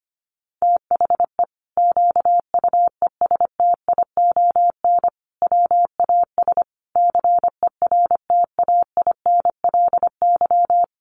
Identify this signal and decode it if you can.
Morse code, heard as a keyed tone, about 25 wpm, 700 Hz. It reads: THE QUESTION WAS CERTAINLY
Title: Sentences from Top 1200 Words 25wpm